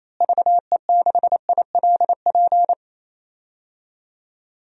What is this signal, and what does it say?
Morse code, heard as a keyed tone, about 28 wpm, 700 Hz. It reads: VE6ILP
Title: DXCC-100 Call Signs - 6 Characters - Repeated 2x - 28wpm